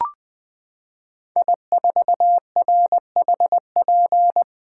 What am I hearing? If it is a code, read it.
Morse code, heard as a keyed tone, about 20 wpm, 700 Hz. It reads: I4RHP
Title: 2023 POTA Hunter Call Signs - 20wpm